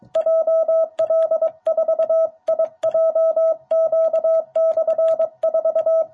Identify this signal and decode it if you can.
Morse code, heard as a keyed tone, about 23 wpm, 660 Hz. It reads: JL4IJQ/4
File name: jh8cbh / 2021cw